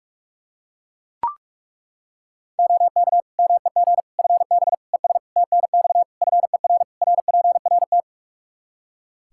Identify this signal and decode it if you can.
Morse code, heard as a keyed tone, about 42 wpm, 700 Hz. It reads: OK MEG FB ES TNX FER RPRT
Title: Practice QSO - Exchange 2 - Info - 45wpm